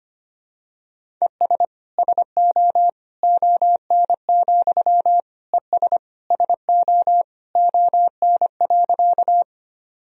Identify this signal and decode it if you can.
Morse code, heard as a keyed tone, about 25 wpm, 700 Hz. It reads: ES SO ON, ES SO ON.
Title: Sentences from Top 100 Words and Abbreviations 25wpm